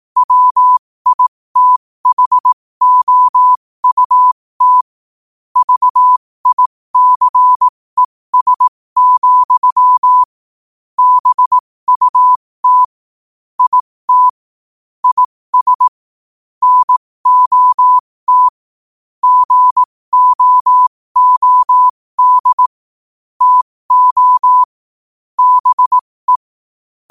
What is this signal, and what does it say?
Morse code, heard as a keyed tone, about 18 wpm, spaced as for 15 wpm, 1000 Hz. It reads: WITHOUT VICES, BUT IT IS NOT GOOD TO BE
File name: qotdfiles / 15wpm